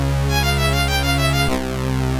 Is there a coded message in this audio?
No.